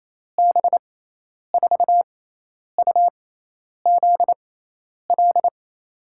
Morse code, heard as a keyed tone, about 28 wpm, spaced as for 10 wpm, 700 Hz. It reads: B4UZL
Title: USA Call Signs - 6 Characters - 6x Character Spacing - 28wpm